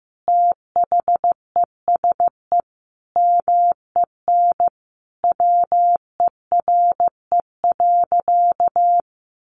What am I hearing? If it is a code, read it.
Morse code, heard as a keyed tone, about 15 wpm, 700 Hz. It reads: THESE MEN WERE.
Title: Sentences from Top 1000 Words 15wpm